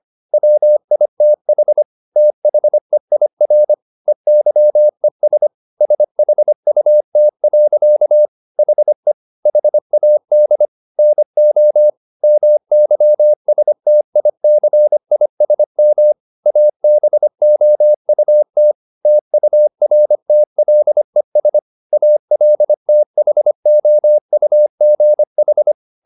Morse code, heard as a keyed tone, about 25 wpm, 600 Hz. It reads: WITH THEIR EYES SHUT. HE HAD NO MYSTICISM ABOUT TURTLES ALTHOUGH
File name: the old man and the sea part 1 25wpm